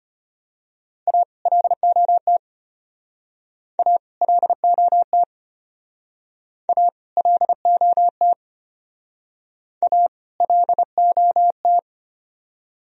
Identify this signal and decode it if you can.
Morse code, heard as a keyed tone, about 33 wpm, 700 Hz. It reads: A LOT A LOT A LOT A LOT